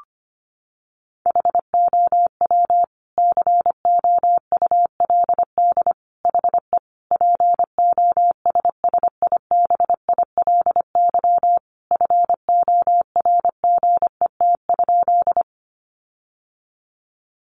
Morse code, heard as a keyed tone, about 25 wpm, 700 Hz. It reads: HOW COULD HE POSSIBILY FORGET?
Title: Sentences from Top 1300 Words 25wpm